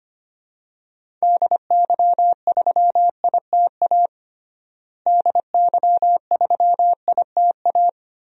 Morse code, heard as a keyed tone, about 25 wpm, 700 Hz. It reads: DY3ITA DY3ITA
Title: DXCC-100 Call Signs - 6 Characters - Repeated 3x - 25wpm